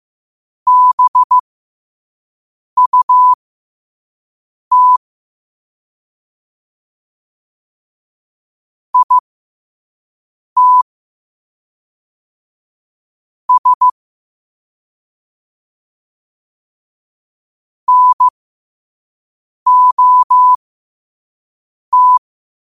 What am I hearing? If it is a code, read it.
Morse code, heard as a keyed tone, about 15 wpm, spaced as for 5 wpm, 1000 Hz. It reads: BUT IT S NOT
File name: qotdfiles / 5wpm